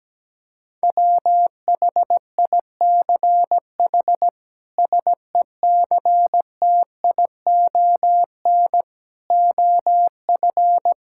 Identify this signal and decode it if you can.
Morse code, heard as a keyed tone, about 17 wpm, 700 Hz. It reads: WHICH SECTION OF
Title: Sentences from Top 1000 Words - Review Part 5 - 17wpm